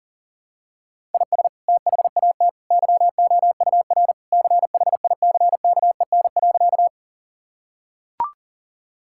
Morse code, heard as a keyed tone, about 40 wpm, 700 Hz. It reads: IS THAT YOUR CHICKEN.
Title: Sentences from Top 1000 Words - Review Part 5 - 40wpm